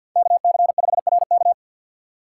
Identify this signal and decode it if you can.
Morse code, heard as a keyed tone, about 52 wpm, 700 Hz. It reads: KC5RK
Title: USA Call Signs - 5 Characters - 6x Character Spacing - 50wpm